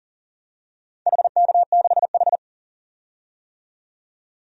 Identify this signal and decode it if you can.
Morse code, heard as a keyed tone, about 40 wpm, 700 Hz. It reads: HK6H
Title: DXCC-100 Call Signs - 4 Characters - 40wpm